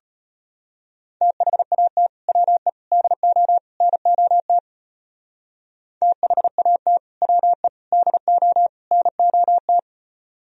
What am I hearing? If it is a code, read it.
Morse code, heard as a keyed tone, about 31 wpm, 700 Hz. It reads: THAT WE DO NOT THAT WE DO NOT
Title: Speed-Racing - Sets of 4 Words 25wpm